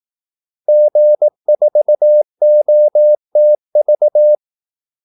Morse code, heard as a keyed tone, about 18 wpm, 600 Hz. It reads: G4OTV
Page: Kaldesignaler 18 wpm